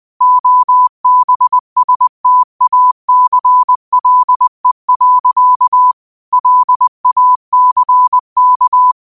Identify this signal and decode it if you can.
Morse code, heard as a keyed tone, about 20 wpm, 1000 Hz. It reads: OBSTACLE. LACK
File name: qotdfiles / 20wpm